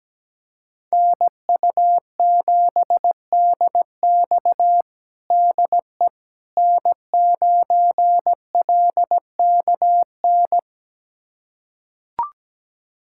Morse code, heard as a keyed tone, about 17 wpm, 700 Hz. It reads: NU7DX DE N9LKN
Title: USA Callsign DE Callsign - 5 Characters - 17wpm